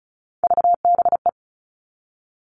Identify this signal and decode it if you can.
Morse code, heard as a keyed tone, about 34 wpm, 700 Hz. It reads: VBE
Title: All Cut Numbers - 3-Digit 35wpm